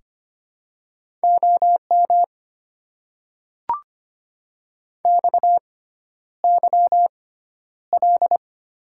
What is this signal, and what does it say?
Morse code, heard as a keyed tone, about 25 wpm, spaced as for 11 wpm, 700 Hz. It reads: OM XYL
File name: Top 90 QSO Elements - 6x Character Spacing - 25wpm